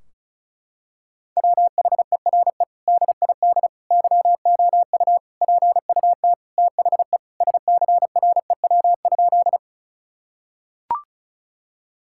Morse code, heard as a keyed tone, about 36 wpm, 700 Hz. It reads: WHERE DID YOU PUT THE SCREW?
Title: Sentences from Top 1600 Words 35wpm